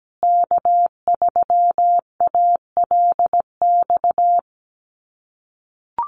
Morse code, heard as a keyed tone, about 17 wpm, 700 Hz. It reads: K3ALX
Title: USA Call Signs Repeated 3x 17wpm